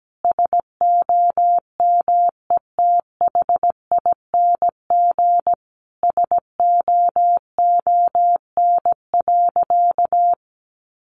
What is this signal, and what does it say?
Morse code, heard as a keyed tone, about 17 wpm, 700 Hz. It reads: SOMETHING SOON.
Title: Sentences from Top 1500 Words 17wpm